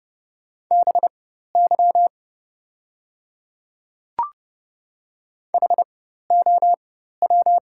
Morse code, heard as a keyed tone, about 30 wpm, spaced as for 14 wpm, 700 Hz. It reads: BY HOW
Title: 4x Character Spacing - Top 100 Words - 30wpm